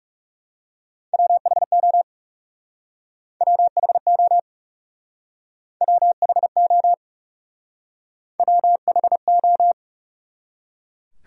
Morse code, heard as a keyed tone, about 38 wpm, 700 Hz. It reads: WHO WHO WHO WHO